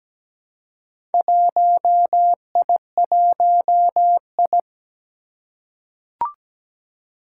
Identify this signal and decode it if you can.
Morse code, heard as a keyed tone, about 17 wpm, 700 Hz. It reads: 1I1I
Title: Four Challenging Letter-Number Combinations 17wpm